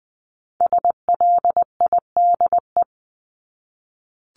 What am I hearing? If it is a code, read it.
Morse code, heard as a keyed tone, about 20 wpm, 700 Hz. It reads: SLIDE
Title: Top 1501-1750 Words 20wpm